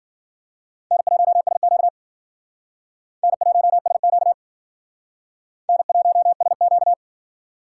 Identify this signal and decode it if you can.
Morse code, heard as a keyed tone, about 51 wpm, 700 Hz. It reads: N1SQ N1SQ N1SQ